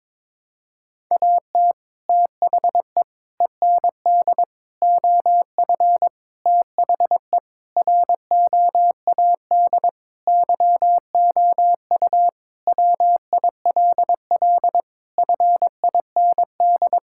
Morse code, heard as a keyed tone, about 22 wpm, 700 Hz. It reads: AT THE END OF THE ROAD YOU WILL FIND
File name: practice / 22wpm / Sentences with 3-Word Phrases - 22wpm